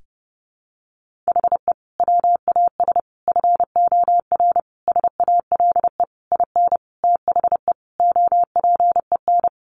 Morse code, heard as a keyed tone, about 30 wpm, 700 Hz. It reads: HE WAS FOR SALE IN THE OPEN
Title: Sentences from Top 1400 Words 30wpm